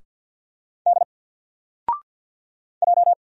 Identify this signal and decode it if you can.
Morse code, heard as a keyed tone, about 49 wpm, 700 Hz. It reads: D J
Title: Single Letter-Number 50wpm